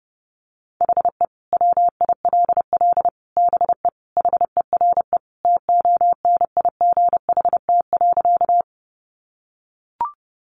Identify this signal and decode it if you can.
Morse code, heard as a keyed tone, about 28 wpm, 700 Hz. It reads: HE WILL BE HERE TONIGHT.
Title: Sentences from Top 600 Words 30wpm